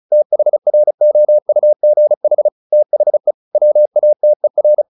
Morse code, heard as a keyed tone, about 35 wpm, 600 Hz. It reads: THROUGH THE WATER
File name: the old man and the sea / the old man and the sea 1 35wpm